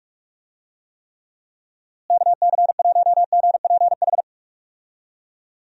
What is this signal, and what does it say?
Morse code, heard as a keyed tone, about 46 wpm, 700 Hz. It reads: KC1GPH
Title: USA Call Signs - 6 Characters - Repeated 3x - 45wpm